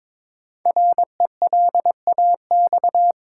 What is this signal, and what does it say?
Morse code, heard as a keyed tone, about 22 wpm, 700 Hz. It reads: RELAX
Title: Sentences from Top 1400 Words - New Word Drill - 22wpm